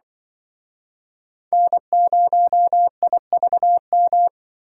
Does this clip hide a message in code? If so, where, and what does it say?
Yes: Morse code from 1.5 to 4.3 s, heard as a keyed tone, about 26 wpm, 700 Hz. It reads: N0IVM